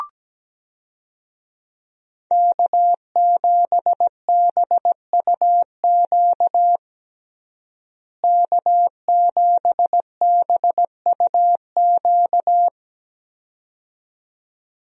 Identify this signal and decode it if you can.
Morse code, heard as a keyed tone, about 17 wpm, 700 Hz. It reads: K7BUQ K7BUQ
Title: USA Call Signs - 5 Characters - Repeated 2x - 17wpm